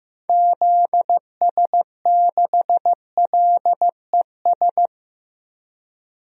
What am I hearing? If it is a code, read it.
Morse code, heard as a keyed tone, about 15 wpm, 700 Hz. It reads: ZS6LES
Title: DXCC-100 Call Signs - 6 Characters - Repeated 3x - 15wpm